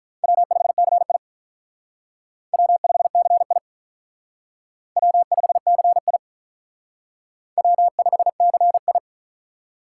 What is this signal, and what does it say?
Morse code, heard as a keyed tone, about 46 wpm, 700 Hz. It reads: W5CI W5CI W5CI W5CI